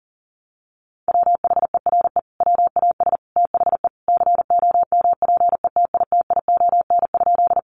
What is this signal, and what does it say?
Morse code, heard as a keyed tone, about 40 wpm, 700 Hz. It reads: WHERE WAS THE COMPETITION?
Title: Sentences from Top 1000 Words 40wpm